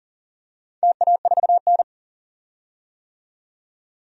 Morse code, heard as a keyed tone, about 40 wpm, 700 Hz. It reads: TA4N
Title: DXCC-100 Call Signs - 4 Characters - 40wpm